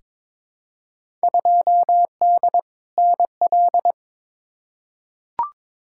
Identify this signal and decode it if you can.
Morse code, heard as a keyed tone, about 22 wpm, 700 Hz. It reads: 2D NL
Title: Field Day Exchanges Encore - 22wpm